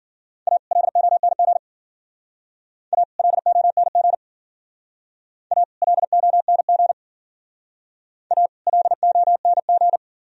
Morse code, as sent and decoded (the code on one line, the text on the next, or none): .- / .-.. --- -. --. / .- / .-.. --- -. --. / .- / .-.. --- -. --. / .- / .-.. --- -. --.
A LONG A LONG A LONG A LONG